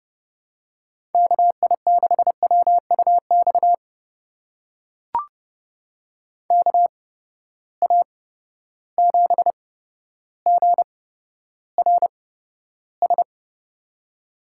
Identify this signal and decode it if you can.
Morse code, heard as a keyed tone, about 30 wpm, spaced as for 13 wpm, 700 Hz. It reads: KI6WUX KA7GRS
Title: USA Call Signs - 6 Characters - 8x Character Spacing - 30wpm